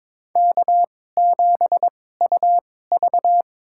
Morse code, heard as a keyed tone, about 22 wpm, spaced as for 16 wpm, 700 Hz. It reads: K7UV
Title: USA Call Signs - 4 Characters - 2x Character Spacing - 22wpm